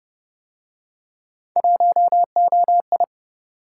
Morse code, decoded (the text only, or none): 1OI